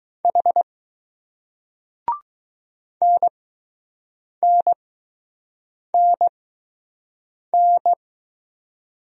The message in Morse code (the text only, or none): H NNNN